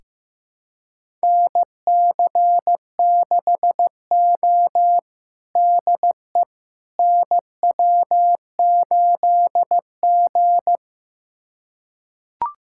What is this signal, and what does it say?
Morse code, heard as a keyed tone, about 15 wpm, 700 Hz. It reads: NC6O DE NW8G
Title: USA Callsign DE Callsign - 4 Characters - 15wpm